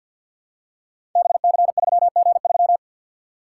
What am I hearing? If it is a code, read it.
Morse code, heard as a keyed tone, about 47 wpm, 700 Hz. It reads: BC3G3
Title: Code Groups - 5 Letter:Numbers - 50wpm